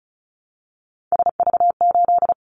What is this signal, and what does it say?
Morse code, heard as a keyed tone, about 36 wpm, 700 Hz. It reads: SV8
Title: Callsign Prefixes for Top 50 DXCC - 35wpm